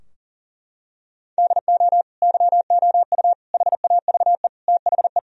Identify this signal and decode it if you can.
Morse code, heard as a keyed tone, about 37 wpm, 700 Hz. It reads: DO YOU HAVE THE
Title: Sentences from Top 1000 Words - Review Part 3 - 40wpm